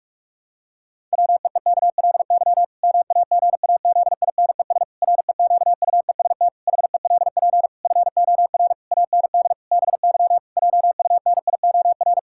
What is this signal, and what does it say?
Morse code, heard as a keyed tone, about 48 wpm, 700 Hz. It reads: WEEKLY MAGAZINES REQUEST HELP FOR AND BY JUNIOR
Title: Pangram Challenge v2 - 45wpm